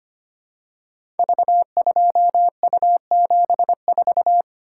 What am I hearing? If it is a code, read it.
Morse code, heard as a keyed tone, about 25 wpm, 700 Hz. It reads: V2U74
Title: Code Groups - 5 Letter:Numbers - 25wpm